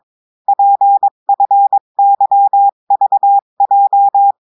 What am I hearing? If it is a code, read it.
Morse code, heard as a keyed tone, about 22 wpm, 800 Hz. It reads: PFYVJ
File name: cours-du-mercredi / audios / cours-853-du-2-mai-2024-etxte chiffre et texte clair-a-20-wpm